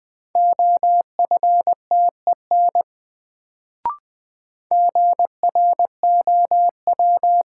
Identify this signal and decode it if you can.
Morse code, heard as a keyed tone, about 20 wpm, 700 Hz. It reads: OFTEN GROW
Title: Sentences from Top 300 Words - New Word Drill - 20wpm